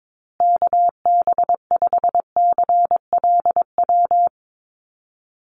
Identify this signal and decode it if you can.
Morse code, heard as a keyed tone, about 22 wpm, 700 Hz. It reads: KB5CLW
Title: USA Call Signs Repeated 2x 22wpm